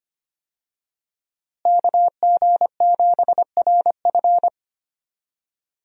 Morse code, heard as a keyed tone, about 25 wpm, 700 Hz. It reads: KG7RF
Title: USA Call Signs - 5 Characters - Repeated 3x - 25wpm